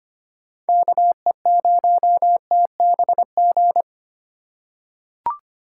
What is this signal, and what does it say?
Morse code, heard as a keyed tone, about 25 wpm, 700 Hz. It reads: KE0TBG
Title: USA Call Signs - 6 Characters - 2x Character Spacing - 25wpm